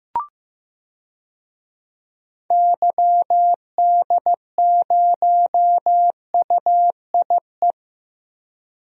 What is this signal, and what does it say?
Morse code, heard as a keyed tone, about 16 wpm, 700 Hz. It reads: YD0UIE
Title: DXCC-100 Call Signs - 6 Characters - Repeated 2x - 15wpm